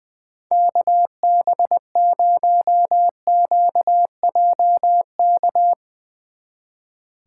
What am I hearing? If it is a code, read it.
Morse code, heard as a keyed tone, about 20 wpm, 700 Hz. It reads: KB0QJK
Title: Speed Racing - USA 2x3 Call Signs 20wpm